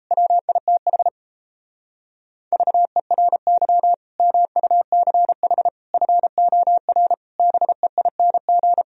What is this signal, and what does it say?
Morse code, heard as a keyed tone, about 38 wpm, 700 Hz. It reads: WITH VERY MUCH FOR BEING